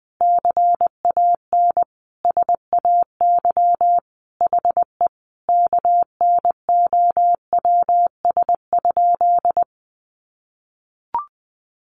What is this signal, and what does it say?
Morse code, heard as a keyed tone, about 20 wpm, 700 Hz. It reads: CAN SAY HE KNOWS?
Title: A Sound Of Thunder 20wpm